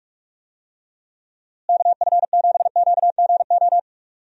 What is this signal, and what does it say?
Morse code, heard as a keyed tone, about 46 wpm, 700 Hz. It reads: KF7QGO
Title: USA Call Signs - 6 Characters - Repeated 2x - 45wpm